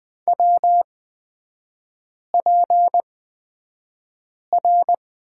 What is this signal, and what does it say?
Morse code, heard as a keyed tone, about 20 wpm, spaced as for 5 wpm, 700 Hz. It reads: WPR
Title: 229 Lesson 37e - Letter x - Three Characters 20wpm